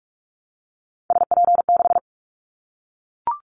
Morse code, heard as a keyed tone, about 44 wpm, 700 Hz. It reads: SP6
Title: Callsign Prefixes for Top 25 DXCC - 45wpm